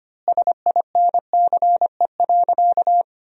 Morse code, heard as a keyed tone, about 25 wpm, 700 Hz. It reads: SINCE.
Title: 4x Word-Spacing - Sentences from Top 400 Words - 25wpm